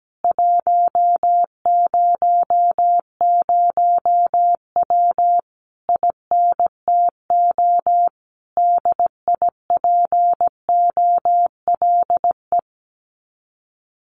Morse code, heard as a keyed tone, about 17 wpm, 700 Hz. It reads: 100W INTO DIPOLE
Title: Common QSO Phrases Encore 17wpm